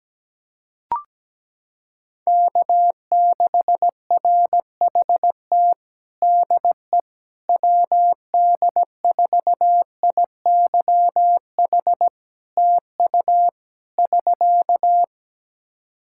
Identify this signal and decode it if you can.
Morse code, heard as a keyed tone, about 17 wpm, 700 Hz. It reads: K6RHT DE WD4IYH TU <SK>
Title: Practice QSO - Exchange 4 - Ending - 17wpm